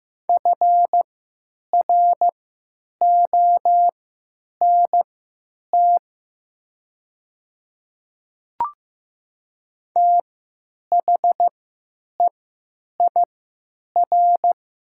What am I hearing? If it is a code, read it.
Morse code, heard as a keyed tone, about 15 wpm, spaced as for 9 wpm, 700 Hz. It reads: FRONT THEIR
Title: 3x Character Spacing - Top 500 Words - 15wpm